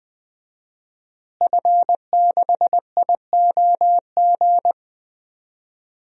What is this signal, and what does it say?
Morse code, heard as a keyed tone, about 20 wpm, 700 Hz. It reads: F6IOG
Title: DXCC-100 Call Signs - 5 Characters - Repeated 2x - 20wpm